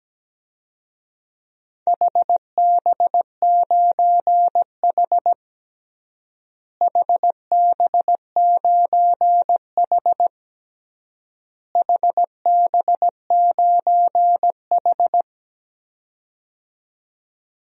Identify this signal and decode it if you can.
Morse code, heard as a keyed tone, about 17 wpm, 700 Hz. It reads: HB9H HB9H HB9H
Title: DXCC-100 Call Signs - 4 Characters - Repeated 3x - 17wpm